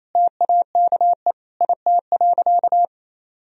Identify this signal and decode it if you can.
Morse code, heard as a keyed tone, about 28 wpm, 700 Hz. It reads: TAKE IT.
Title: Sentences from Top 1000 Words - Review Part 4 - 28wpm